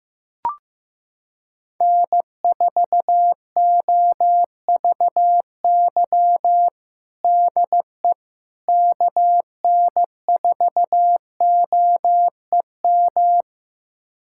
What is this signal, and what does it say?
Morse code, heard as a keyed tone, about 15 wpm, 700 Hz. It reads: N4OVY DE KN4OEM
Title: USA Callsign DE Callsign 15wpm